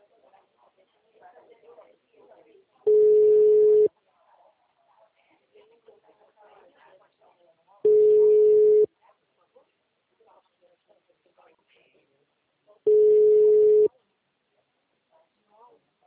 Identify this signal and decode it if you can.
Ringback tone (Europe, 425 Hz), with speech.
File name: out- -2009- - - .35567